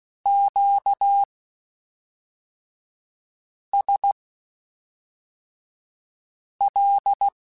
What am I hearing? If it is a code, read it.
Morse code, heard as a keyed tone, about 16 wpm, spaced as for 3 wpm, 800 Hz. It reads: QSL
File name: (spaced) / Q-Calls Spaced-mp3s / Q Calls sp Once @5